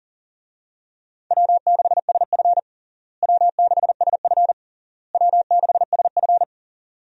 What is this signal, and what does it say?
Morse code, heard as a keyed tone, about 37 wpm, 700 Hz. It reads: W6SF W6SF W6SF
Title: USA Call Signs - 4 Characters - Repeated 3x - 40wpm